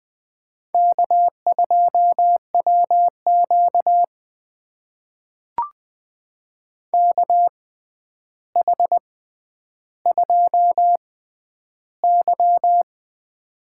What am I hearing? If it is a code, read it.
Morse code, heard as a keyed tone, about 20 wpm, spaced as for 10 wpm, 700 Hz. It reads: K2WQ KH2Y
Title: USA Call Signs - 4 Characters - 6x Character Spacing - 20wpm